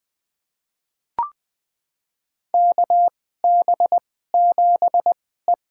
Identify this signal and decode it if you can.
Morse code, heard as a keyed tone, about 20 wpm, spaced as for 15 wpm, 700 Hz. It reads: KB7E
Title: USA Call Signs - 4 Characters - 2x Character Spacing - 20wpm